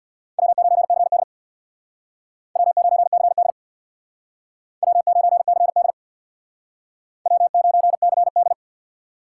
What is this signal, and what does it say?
Morse code, heard as a keyed tone, about 62 wpm, 700 Hz. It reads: W9CD W9CD W9CD W9CD
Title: Speed Racing - USA Call Signs 50wpm